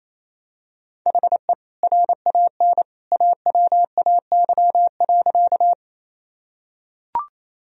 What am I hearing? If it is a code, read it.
Morse code, heard as a keyed tone, about 28 wpm, 700 Hz. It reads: HE RAN AWAY.
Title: Sentences from Top 1000 Words - Review Part 3 - 28wpm